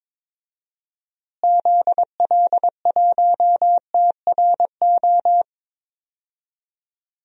Morse code, heard as a keyed tone, about 22 wpm, 700 Hz. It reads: ZL1TRO
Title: DXCC-100 Call Signs - 6 Characters - Repeated 2x - 22wpm